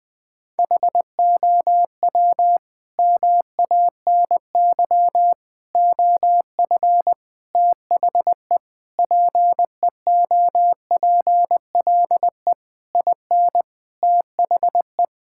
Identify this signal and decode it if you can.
Morse code, heard as a keyed tone, about 21 wpm, 700 Hz. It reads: HOW MANY OF THE PEOPLE IN THE
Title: Sentences from Top 1000 Words - Review Part 6 - 20wpm